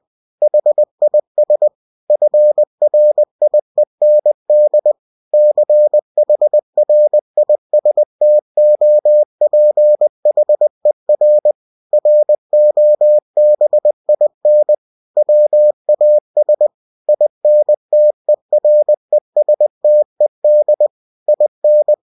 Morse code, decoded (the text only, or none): HIS FRIEND CHRISTOPHER ROBIN WAS INTERESTED IN